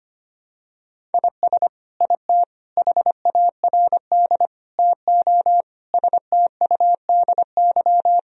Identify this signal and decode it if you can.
Morse code, heard as a keyed tone, about 25 wpm, 700 Hz. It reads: IS IT HARD TO STUDY
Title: Sentences from Top 1000 Words - Review Part 3 - 25wpm